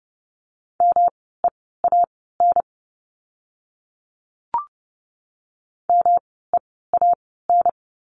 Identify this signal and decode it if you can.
Morse code, heard as a keyed tone, about 30 wpm, spaced as for 17 wpm, 700 Hz. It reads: MEAN MEAN
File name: course / 30:10wpm / 108 Lesson 18f - Letter m - Words 30wpm